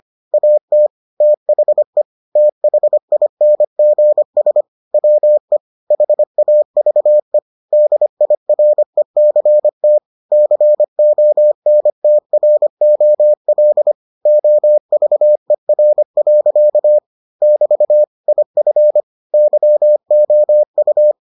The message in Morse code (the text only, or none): AT THE THINGS WE HAVE DIRECT CONTROL OVER. <BT> IF YOU